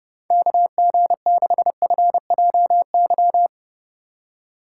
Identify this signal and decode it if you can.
Morse code, heard as a keyed tone, about 31 wpm, 700 Hz. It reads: KG6FJY
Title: USA Call Signs - 6 Characters - 2x Character Spacing - 30wpm